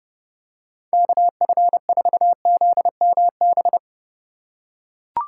Morse code, heard as a keyed tone, about 31 wpm, 700 Hz. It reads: KF4ZMB